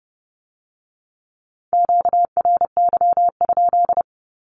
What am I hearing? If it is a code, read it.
Morse code, heard as a keyed tone, about 30 wpm, 700 Hz. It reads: QRY?
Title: Q-Codes with Spoken Meaning - 30wpm